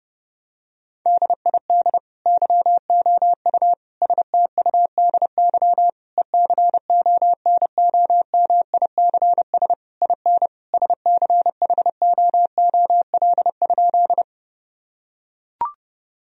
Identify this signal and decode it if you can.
Morse code, heard as a keyed tone, about 30 wpm, 700 Hz. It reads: DID YOU STUDY ECONOMICS IN SCHOOL?
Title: Sentences from Top 1000 Words - Review Part 3 - 30wpm